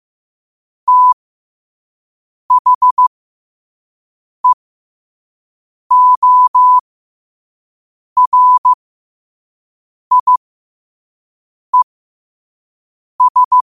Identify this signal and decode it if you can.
Morse code, heard as a keyed tone, about 15 wpm, spaced as for 5 wpm, 1000 Hz. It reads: THEORIES
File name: qotdfiles / 5wpm